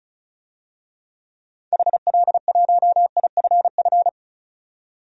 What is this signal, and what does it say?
Morse code, heard as a keyed tone, about 34 wpm, 700 Hz. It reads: HL1IFF
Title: DXCC-100 Call Signs - 6 Characters - 35wpm